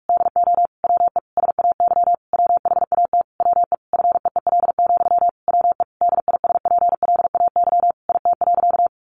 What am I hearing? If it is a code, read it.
Morse code, heard as a keyed tone, about 44 wpm, 700 Hz. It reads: DO WE SAY WHAT WE FEEL, WE DISPLAY IT.
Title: Sentences from Top 1200 Words 45wpm